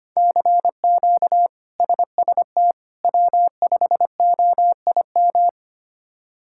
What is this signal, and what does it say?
Morse code, heard as a keyed tone, about 25 wpm, 700 Hz. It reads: CQ SST W5OIM
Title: K1USN SST QSO Practice - 25wpm